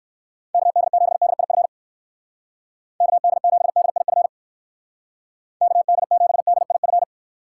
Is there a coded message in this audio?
Yes: Morse code from 0.5 to 7.0 s, heard as a keyed tone, about 59 wpm, 700 Hz. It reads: KD7DIF KD7DIF KD7DIF